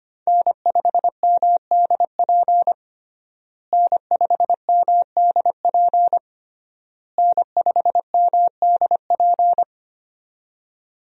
Morse code, heard as a keyed tone, about 25 wpm, 700 Hz. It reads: N5MDP N5MDP N5MDP
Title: USA Call Signs - 5 Characters - Repeated 3x - 25wpm